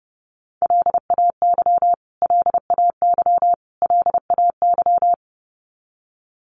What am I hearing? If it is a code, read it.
Morse code, heard as a keyed tone, about 28 wpm, 700 Hz. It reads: LAY LAY LAY
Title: Top 500 Words Repeated 30wpm